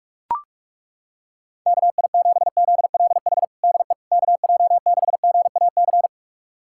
Morse code, heard as a keyed tone, about 41 wpm, 690 Hz. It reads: KI7ZLH DE KJ6GAC
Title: USA Callsign DE Callsign - 6 Characters - 45wpm